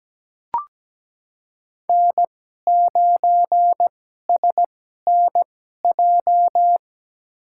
Morse code, heard as a keyed tone, about 17 wpm, spaced as for 12 wpm, 700 Hz. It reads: N9SNJ
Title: USA Call Signs - 5 Characters - 2x Character Spacing - 17wpm